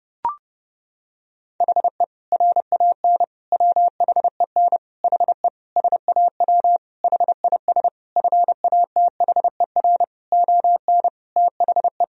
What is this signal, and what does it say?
Morse code, heard as a keyed tone, about 32 wpm, 700 Hz. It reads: HE RAN WHEN HE SAW HIS FATHER ON THE
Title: Sentences from Top 500 Words and Abbreviations 30wpm